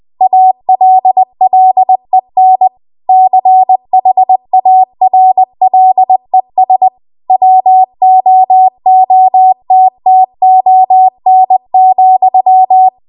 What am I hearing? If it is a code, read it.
Morse code, heard as a keyed tone, about 20 wpm, 750 Hz. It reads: ALLEN CHARLES WOOTTON,
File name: Archive / 20 WPM / 20WPM